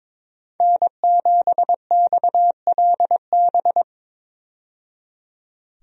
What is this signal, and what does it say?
Morse code, heard as a keyed tone, about 22 wpm, 700 Hz. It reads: N7XLB
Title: USA Call Signs - 5 Characters - 22wpm